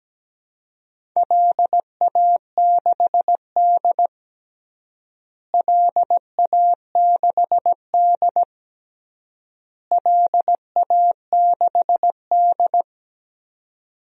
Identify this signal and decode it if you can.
Morse code, heard as a keyed tone, about 17 wpm, 700 Hz. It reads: LA6D LA6D LA6D
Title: DXCC-100 Call Signs - 4 Characters - Repeated 3x - 17wpm